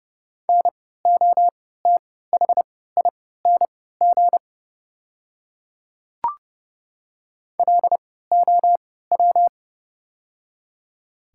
Morse code, heard as a keyed tone, about 30 wpm, spaced as for 17 wpm, 700 Hz. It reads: NOTHING LOW
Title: 3x Character Spacing - Top 500 Words - 30wpm